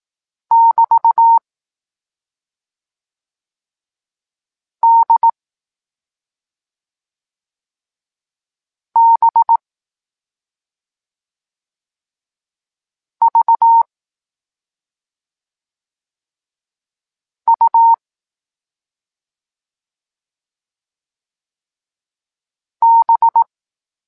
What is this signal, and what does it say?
Morse code, heard as a keyed tone, about 18 wpm, 920 Hz. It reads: <BT> D B V U B
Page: Tekst 14 – doskonalenie odbioru poznanych liter